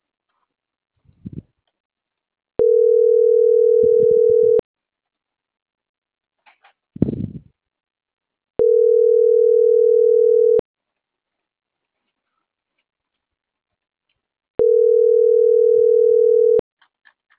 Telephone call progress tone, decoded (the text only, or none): ringback tone